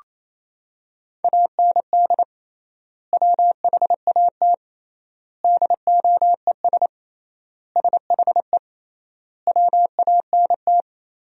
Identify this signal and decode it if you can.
Morse code, heard as a keyed tone, about 28 wpm, 700 Hz. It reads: AND WHAT DOES SHE WANT
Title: 3x Word-Spacing - Sentences from Top 400 Words - 28wpm